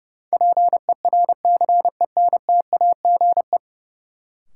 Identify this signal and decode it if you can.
Morse code, heard as a keyed tone, about 30 wpm, 700 Hz. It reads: PERCENTAGE
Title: Sentences from Top 1300 Words - New Word Drill Alt - 30wpm